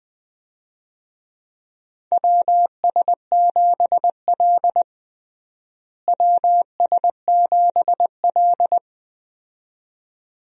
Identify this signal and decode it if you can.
Morse code, heard as a keyed tone, about 20 wpm, 700 Hz. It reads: WS7L WS7L
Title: USA Call Signs - 4 Characters - Repeated 2x - 20wpm